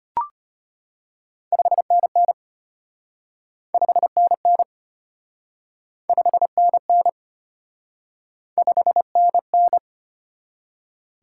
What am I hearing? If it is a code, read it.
Morse code, heard as a keyed tone, about 34 wpm, 700 Hz. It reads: 5NN 5NN 5NN 5NN